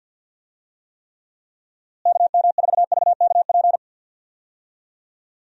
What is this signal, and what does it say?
Morse code, heard as a keyed tone, about 52 wpm, 700 Hz. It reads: KM4VKP
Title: USA Call Signs - 6 Characters - Repeated 2x - 50wpm